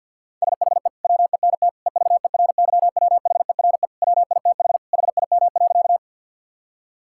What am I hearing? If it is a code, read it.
Morse code, heard as a keyed tone, about 50 wpm, 700 Hz. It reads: SHE WENT EVERYWHERE WITH HIM.
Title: Sentences from Top 1600 Words 50wpm